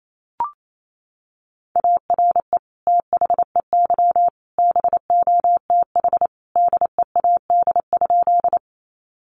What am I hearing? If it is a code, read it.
Morse code, heard as a keyed tone, about 28 wpm, 700 Hz. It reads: ARE THEY BOTH DEAD?
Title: Sentences from Top 1500 Words 28wpm